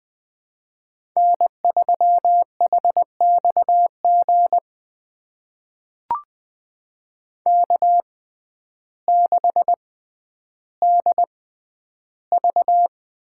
Morse code, heard as a keyed tone, about 20 wpm, spaced as for 11 wpm, 700 Hz. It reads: N3HXG K6DV
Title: USA Call Signs - 6x Character Spacing - 20wpm